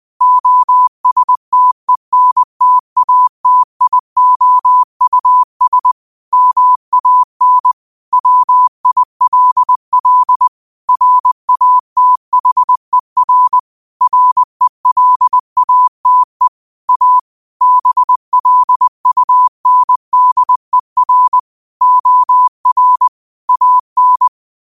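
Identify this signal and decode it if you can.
Morse code, heard as a keyed tone, about 20 wpm, 1000 Hz. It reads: OSTENTATIOUS MAN WILL RATHER RELATE A BLUNDER OR AN
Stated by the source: Morse code quote of the day at 20 wpm.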